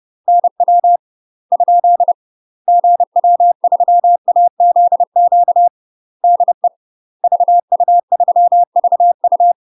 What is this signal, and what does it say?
Morse code, heard as a keyed tone, about 30 wpm, 700 Hz. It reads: NW ? GW3AZQ DE VU3VU
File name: UK MASTER QSOs TRAINING / 30WPM / UK MASTER QSOs 0000